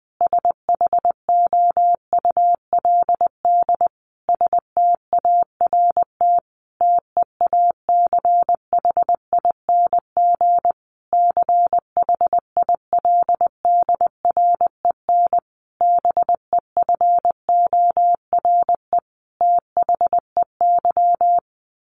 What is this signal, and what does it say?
Morse code, heard as a keyed tone, about 20 wpm, 700 Hz. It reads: SHOULD START TEACHING CHILDREN BEFORE THEY
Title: Sentences from Top 1100 Words 20wpm